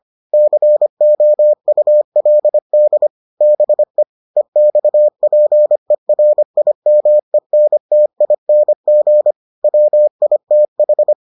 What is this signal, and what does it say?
Morse code, heard as a keyed tone, about 25 wpm, 600 Hz. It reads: COULD BE EXPERIMENTING WITH